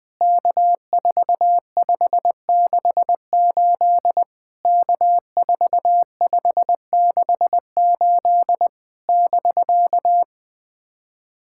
Morse code, heard as a keyed tone, about 20 wpm, 700 Hz. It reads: K4568 K4568 <BK>
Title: POTA QSO Practice - 20wpm